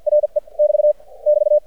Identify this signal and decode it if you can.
Morse code, heard as a keyed tone, about 52 wpm, spaced as for 42 wpm, 600 Hz. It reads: RE <BT> <BT>
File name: archived / CRSnet